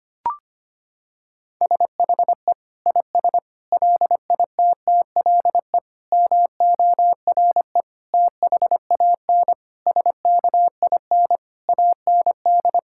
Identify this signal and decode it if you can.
Morse code, heard as a keyed tone, about 25 wpm, 700 Hz. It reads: SHE IS LITTLE MORE THAN SKIN AND
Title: Sentences from Top 1600 Words 25wpm